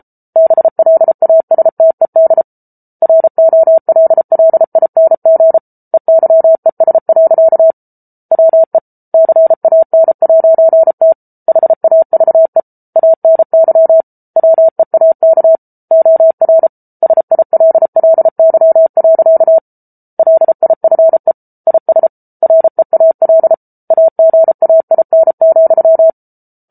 Morse code, heard as a keyed tone, about 34 wpm, 650 Hz. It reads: BLASTED ROLLING EYES. WE CAN'T HAVE ANY WEAK OR SILLY. LIFE IS REAL AGAIN,